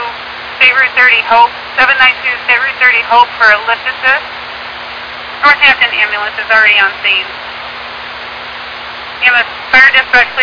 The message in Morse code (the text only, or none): none